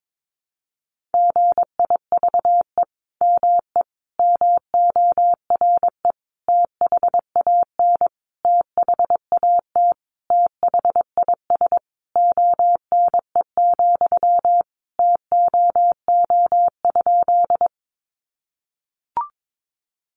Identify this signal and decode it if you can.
Morse code, heard as a keyed tone, about 22 wpm, 700 Hz. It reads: GIVE ME MORE THAN THAT THIS ONE, TOO?
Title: Sentences from Top 200 Words 22wpm